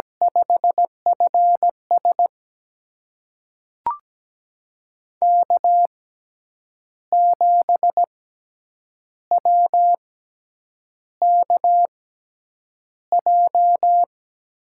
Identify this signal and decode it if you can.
Morse code, heard as a keyed tone, about 17 wpm, spaced as for 7 wpm, 700 Hz. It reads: 5FS K7WKJ
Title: USA Call Signs - 6x Character Spacing - 17wpm